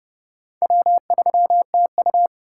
Morse code, heard as a keyed tone, about 38 wpm, 700 Hz. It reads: W3TU